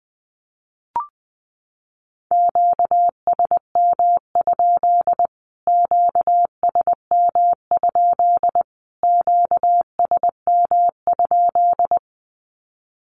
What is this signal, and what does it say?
Morse code, heard as a keyed tone, about 20 wpm, 700 Hz. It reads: QSM? QSM? QSM?
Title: Q-Codes Repeated 3x 20wpm